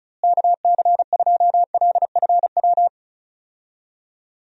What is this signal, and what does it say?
Morse code, heard as a keyed tone, about 36 wpm, 700 Hz. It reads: KC2LFW
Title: USA Call Signs - 6 Characters - Repeated 2x - 35wpm